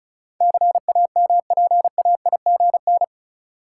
Morse code, heard as a keyed tone, about 34 wpm, 700 Hz. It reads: CAMPAIGN
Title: Sentences from Top 1200 Words - New Word Drill - 35wpm